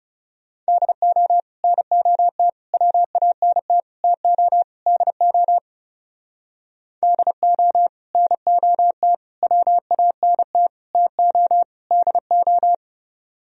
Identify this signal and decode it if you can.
Morse code, heard as a keyed tone, about 37 wpm, 700 Hz. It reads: DO NOT WANT TO DO DO NOT WANT TO DO
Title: Speed-Racing - Sets of 5 Words 30wpm